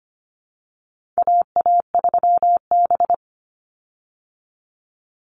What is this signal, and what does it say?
Morse code, heard as a keyed tone, about 25 wpm, 700 Hz. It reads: AA3B
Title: Call Signs Repeated 2x - 25wpm